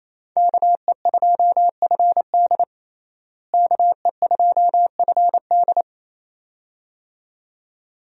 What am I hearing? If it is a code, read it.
Morse code, heard as a keyed tone, about 28 wpm, 700 Hz. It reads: KE2FD KE2FD
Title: USA Call Signs - 5 Characters - Repeated 2x - 28wpm